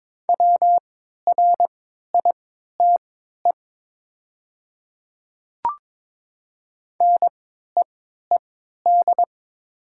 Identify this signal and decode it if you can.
Morse code, heard as a keyed tone, about 22 wpm, spaced as for 12 wpm, 700 Hz. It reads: WRITE NEED
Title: 3x Character Spacing - Top 200 Words - 22wpm